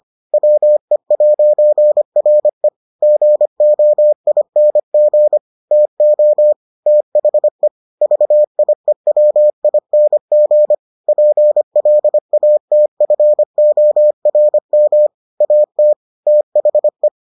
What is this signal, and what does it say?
Morse code, heard as a keyed tone, about 25 wpm, 600 Hz. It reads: WE'RE GOING TO THE VIEWING PLATFORM AT THE